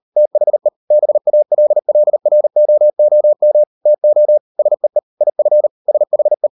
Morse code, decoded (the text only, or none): THE BALLROOM TO SEE IF SHE